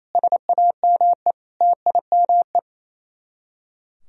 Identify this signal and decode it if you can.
Morse code, heard as a keyed tone, about 34 wpm, 700 Hz. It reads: SAME TIME
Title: Speed-Racing - Sets of 5 Words 28wpm